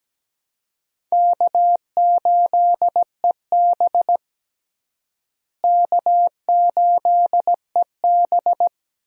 Morse code, heard as a keyed tone, about 17 wpm, 700 Hz. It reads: K8EB K8EB
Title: USA Call Signs - 4 Characters - Repeated 2x - 17wpm